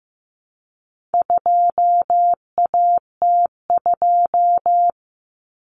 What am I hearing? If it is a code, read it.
Morse code, heard as a keyed tone, about 15 wpm, 700 Hz. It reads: 2AT2
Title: Common Cut Numbers - 4-Digit 15wpm